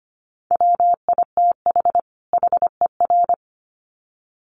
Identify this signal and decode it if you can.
Morse code, heard as a keyed tone, about 25 wpm, 700 Hz. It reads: WITH HER
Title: Sets of 2 Words 25wpm